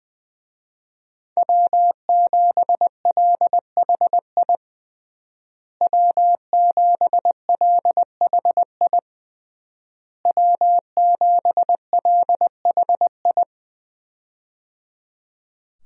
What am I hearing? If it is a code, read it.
Morse code, heard as a keyed tone, about 21 wpm, 700 Hz. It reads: W7LHI W7LHI W7LHI
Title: USA Call Signs - 5 Characters - Repeated 3x - 20wpm